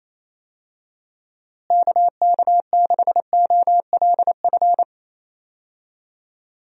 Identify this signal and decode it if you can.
Morse code, heard as a keyed tone, about 28 wpm, 700 Hz. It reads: KK6OLF
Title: USA Call Signs - 6 Characters - Repeated 3x - 28wpm